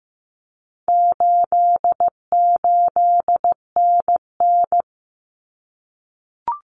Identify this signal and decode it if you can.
Morse code, heard as a keyed tone, about 15 wpm, 700 Hz. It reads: 88NN
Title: Common Cut Numbers - 4-Digit 15wpm